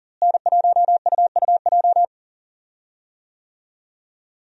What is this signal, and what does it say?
Morse code, heard as a keyed tone, about 37 wpm, 700 Hz. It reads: N1UUJ
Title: USA Call Signs - 5 Characters - Repeated 3x - 40wpm